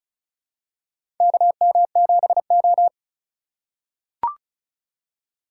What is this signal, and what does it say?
Morse code, heard as a keyed tone, about 34 wpm, 700 Hz. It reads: KM7O
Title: USA Call Signs - 4 Characters - 2x Character Spacing - 35wpm